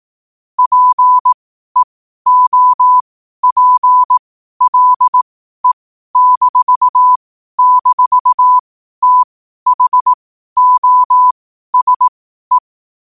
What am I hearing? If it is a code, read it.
Morse code, heard as a keyed tone, about 18 wpm, spaced as for 12 wpm, 1000 Hz. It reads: PEOPLE--THOSE
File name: 12wpm